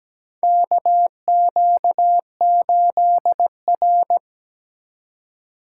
Morse code, heard as a keyed tone, about 17 wpm, 700 Hz. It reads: KQ8R
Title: USA Call Signs - 4x Character Spacing - 17wpm